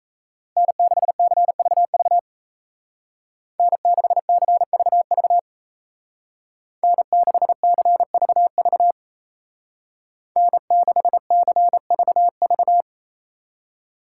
Morse code, heard as a keyed tone, about 34 wpm, 700 Hz. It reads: N6CVV N6CVV N6CVV N6CVV
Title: Speed Racing - USA Call Signs 28wpm